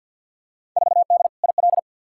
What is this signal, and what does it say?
Morse code, heard as a keyed tone, about 52 wpm, 700 Hz. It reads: 4D IL
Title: Field Day Exchanges Encore - 50wpm